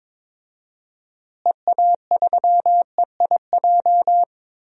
Morse code, heard as a keyed tone, about 22 wpm, 700 Hz. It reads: EA3EIJ
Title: DXCC-100 Call Signs - 6 Characters - Repeated 3x - 22wpm